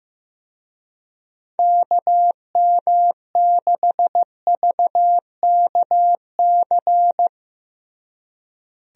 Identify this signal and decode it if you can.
Morse code, heard as a keyed tone, about 15 wpm, 700 Hz. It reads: KM6VKC
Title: USA Call Signs - 6 Characters - Repeated 2x - 15wpm